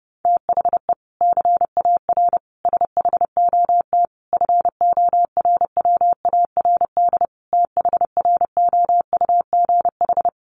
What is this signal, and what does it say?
Morse code, heard as a keyed tone, about 30 wpm, 700 Hz. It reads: THE CAR SHOT FORWARD THROUGH
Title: Sentences from Top 1300 Words 30wpm